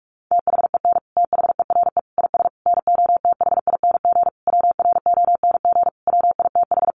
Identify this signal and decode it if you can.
Morse code, heard as a keyed tone, about 44 wpm, 700 Hz. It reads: THEN THERE IS NOTHING WRONG WITH
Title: Sentences from Top 900 Words 45wpm